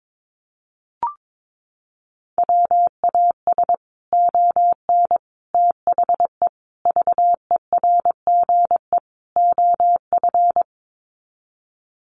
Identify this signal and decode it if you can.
Morse code, heard as a keyed tone, about 22 wpm, 700 Hz. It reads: WAS ON THE VERGE OF
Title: Sets of 5 Words Encore 22wpm